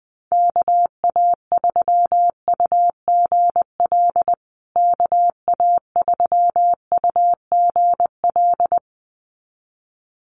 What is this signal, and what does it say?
Morse code, heard as a keyed tone, about 20 wpm, 700 Hz. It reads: KA3UGL KA3UGL
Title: USA Call Signs Repeated 2x 20wpm